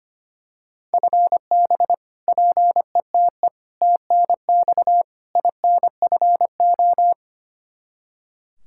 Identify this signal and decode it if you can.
Morse code, heard as a keyed tone, about 25 wpm, 700 Hz. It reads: FB PETE TNX INFO
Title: Practice QSO - Exchange 3 - More Info - 25wpm